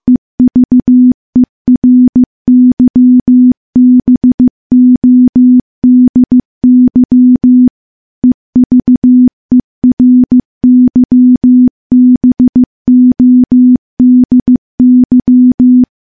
Morse code, heard as a keyed tone, about 15 wpm, 250 Hz. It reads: EVERYBODY EVERYBODY
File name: morse / morse-training-15WPM